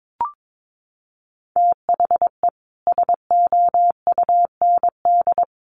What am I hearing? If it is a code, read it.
Morse code, heard as a keyed tone, about 22 wpm, 700 Hz. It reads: THE SOUND
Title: Sentences from Top 1200 Words 22wpm